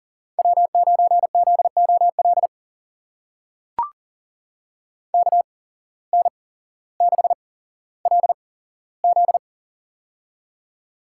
Morse code, heard as a keyed tone, about 38 wpm, spaced as for 17 wpm, 700 Hz. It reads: W9ZOL KN6LZ
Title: USA Call Signs - 5 Characters - 8x Character Spacing - 40wpm